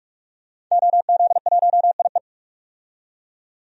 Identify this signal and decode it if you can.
Morse code, heard as a keyed tone, about 43 wpm, 700 Hz. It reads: OZ1IE
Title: DXCC-100 Call Signs - 5 Characters - 45wpm